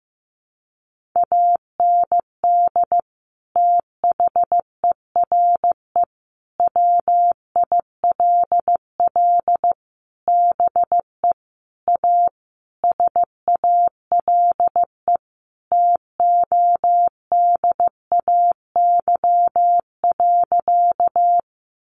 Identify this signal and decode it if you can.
Morse code, heard as a keyed tone, about 15 wpm, 700 Hz. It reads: AND THERE WILL BE A SALE TODAY.
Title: Sentences from Top 1400 Words 15wpm